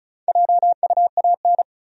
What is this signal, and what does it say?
Morse code, heard as a keyed tone, about 34 wpm, 700 Hz. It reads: JUAN
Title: USA Common Names - 35wpm